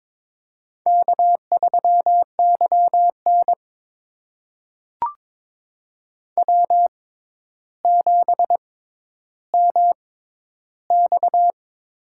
Morse code, heard as a keyed tone, about 22 wpm, spaced as for 11 wpm, 700 Hz. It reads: K3YN W7MX
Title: USA Call Signs - 4 Characters - 6x Character Spacing - 22wpm